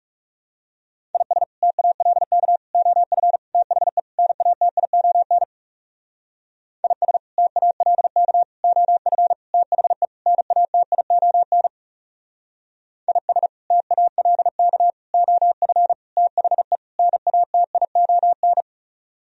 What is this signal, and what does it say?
Morse code, heard as a keyed tone, about 36 wpm, 700 Hz. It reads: IS TALK OF THE NATION IS TALK OF THE NATION IS TALK OF THE NATION
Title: Speed-Racing - Sets of 5 Words 30wpm